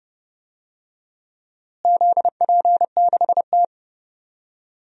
Morse code, heard as a keyed tone, about 30 wpm, 700 Hz. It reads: ZP6T
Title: DXCC-100 Call Signs - 4 Characters - Repeated 3x - 30wpm